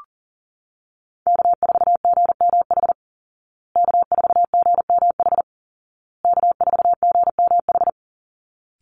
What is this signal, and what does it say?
Morse code, heard as a keyed tone, about 40 wpm, 700 Hz. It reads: K4GMH K4GMH K4GMH
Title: Call Signs Repeated 2x - 40wpm